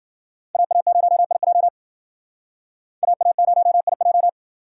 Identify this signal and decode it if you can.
Morse code, heard as a keyed tone, about 51 wpm, 700 Hz. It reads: AA0IJ AA0IJ